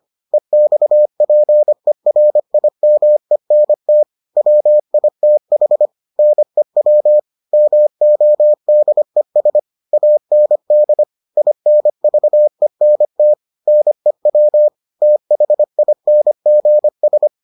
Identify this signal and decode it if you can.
Morse code, heard as a keyed tone, about 25 wpm, 600 Hz. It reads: EXPERIMENT WITH NEW MODES AND INVENT NEW THINGS